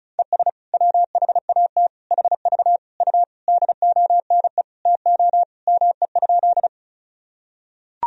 Morse code, heard as a keyed tone, about 36 wpm, 700 Hz. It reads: ES WHAT HV U DONE TO ME?
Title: Sentences from Top 500 Words and Abbreviations 35wpm